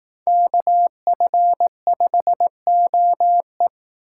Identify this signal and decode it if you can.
Morse code, heard as a keyed tone, about 18 wpm, 700 Hz. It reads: KF5OE
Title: Speed Racing - USA 2x2 Call Signs 15wpm